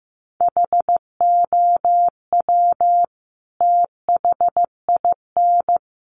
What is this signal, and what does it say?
Morse code, heard as a keyed tone, about 15 wpm, 700 Hz. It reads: HOW THIN
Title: Sentences from Top 1200 Words 15wpm